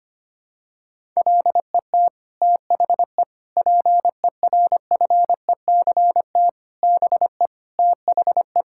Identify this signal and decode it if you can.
Morse code, heard as a keyed tone, about 25 wpm, 700 Hz. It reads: LET THE PERFECT BE THE
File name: Sets of 9 Words - 25wpm